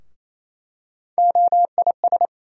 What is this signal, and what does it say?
Morse code, heard as a keyed tone, about 28 wpm, 700 Hz. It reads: OIS
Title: USA Call Sign Postfixes - 3 Letters 28wpm